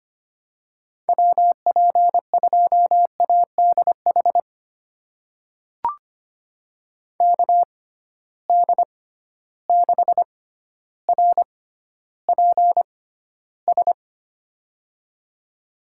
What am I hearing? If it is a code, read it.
Morse code, heard as a keyed tone, about 25 wpm, spaced as for 13 wpm, 700 Hz. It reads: WP2ADH KD6RPS
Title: USA Call Signs - 6 Characters - 6x Character Spacing - 25wpm